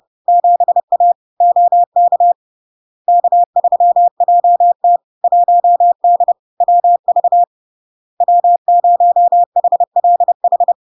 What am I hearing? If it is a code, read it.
Morse code, heard as a keyed tone, about 30 wpm, 700 Hz. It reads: 7A OK K3JT 1D WV W0HLH
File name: LARK-FD-2022-Log-in-CW-30wpm-700-Hz